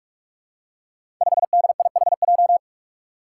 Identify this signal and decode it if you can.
Morse code, heard as a keyed tone, about 44 wpm, 700 Hz. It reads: 5DIHJ
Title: Five Challenging Letter-Number Combinations 45wpm